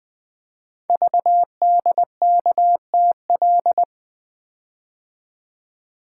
Morse code, heard as a keyed tone, about 20 wpm, 700 Hz. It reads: VDKTL
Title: Code Groups - 5 Letters - 20wpm